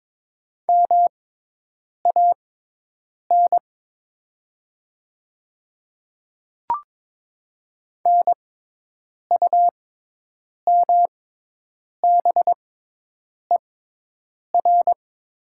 Morse code, heard as a keyed tone, about 22 wpm, spaced as for 8 wpm, 700 Hz. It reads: MAN NUMBER
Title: 6x Character Spacing - Top 300 Words - 22wpm